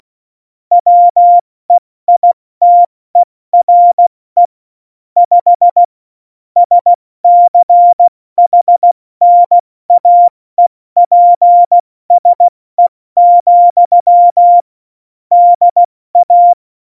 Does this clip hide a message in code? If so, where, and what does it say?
Yes: Morse code from 0.7 to 16.5 s, heard as a keyed tone, about 16 wpm, spaced as for 14 wpm, 700 Hz. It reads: WEITERE 5 SCHNAEPSE, DA